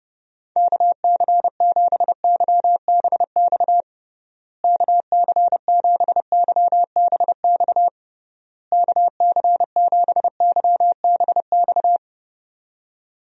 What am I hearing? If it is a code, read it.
Morse code, heard as a keyed tone, about 29 wpm, 700 Hz. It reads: KC7YBX KC7YBX KC7YBX
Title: USA Call Signs - 6 Characters - Repeated 3x - 30wpm